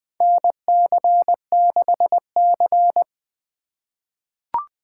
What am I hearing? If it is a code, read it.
Morse code, heard as a keyed tone, about 21 wpm, 700 Hz. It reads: NC6C
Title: USA Call Signs - 4 Characters - 6x Character Spacing - 20wpm